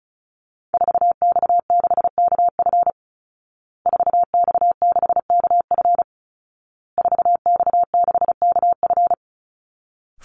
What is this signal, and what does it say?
Morse code, heard as a keyed tone, about 34 wpm, 700 Hz. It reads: 4X6KF 4X6KF 4X6KF
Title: Call Signs Repeated 2x - 35wpm